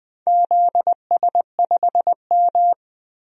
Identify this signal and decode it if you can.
Morse code, heard as a keyed tone, about 21 wpm, 700 Hz. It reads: ZS5M
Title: DXCC-100 Call Signs - 4 Characters - 20wpm